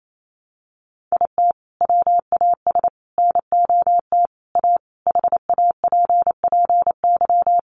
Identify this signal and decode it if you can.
Morse code, heard as a keyed tone, about 28 wpm, 700 Hz. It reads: IT WAS NOT A HAPPY
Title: Sentences from Top 1000 Words 28wpm